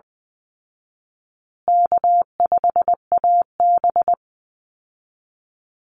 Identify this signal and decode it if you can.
Morse code, heard as a keyed tone, about 20 wpm, 700 Hz. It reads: K5AB
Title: Easy Call Signs Repeated 3x - 20wpm